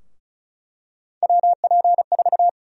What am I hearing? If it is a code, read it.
Morse code, heard as a keyed tone, about 36 wpm, 700 Hz. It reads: WP4
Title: USA Call Sign Prefixes - Repeated 2x - 35wpm